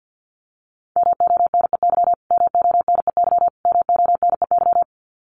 Morse code, heard as a keyed tone, about 50 wpm, 700 Hz. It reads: MONEY MONEY MONEY
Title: Top 500 Words Repeated 50wpm